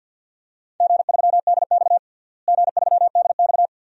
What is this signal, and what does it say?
Morse code, heard as a keyed tone, about 46 wpm, 700 Hz. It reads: G3DX G3DX
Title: DXCC-100 Call Signs - 4 Characters - Repeated 2x - 50wpm